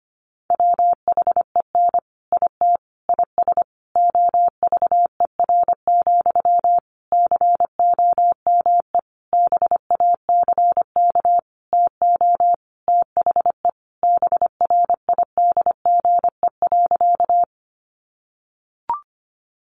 Morse code, heard as a keyed tone, about 25 wpm, 700 Hz. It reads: WHEN IT IS OVER, COME BACK TO THE BRIDGE.
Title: Sentences from Top 1200 Words 25wpm